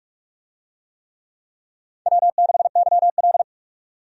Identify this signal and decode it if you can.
Morse code, heard as a keyed tone, about 41 wpm, 700 Hz. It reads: W6YL
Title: USA Call Signs - 4 Characters - 45wpm